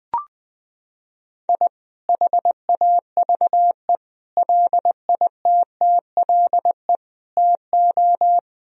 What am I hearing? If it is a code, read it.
Morse code, heard as a keyed tone, about 20 wpm, 700 Hz. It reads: I HAVE LITTLE TO
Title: Sentences from Top 1000 Words - Review Part 1 - 20wpm